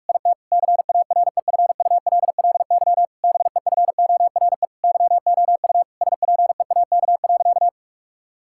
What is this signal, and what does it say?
Morse code, heard as a keyed tone, about 46 wpm, 700 Hz. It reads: IT CAREFULLY BEFORE YOU SPEAK.
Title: Sentences from Top 1600 Words 45wpm